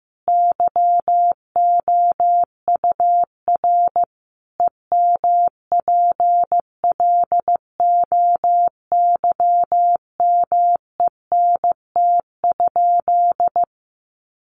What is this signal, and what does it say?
Morse code, heard as a keyed tone, about 14 wpm, 700 Hz. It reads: YOUR EMPLOYMENT?
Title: Sentences from Top 1500 Words 15wpm